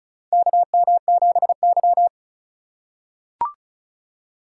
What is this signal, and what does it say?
Morse code, heard as a keyed tone, about 36 wpm, 700 Hz. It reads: KM7Y
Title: USA Call Signs - 4 Characters - 4x Character Spacing - 35wpm